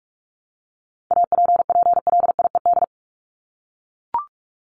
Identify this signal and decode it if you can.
Morse code, heard as a keyed tone, about 44 wpm, 700 Hz. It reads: APPLIED
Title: ED Suffix - Top 100 Words with Base Words 45wpm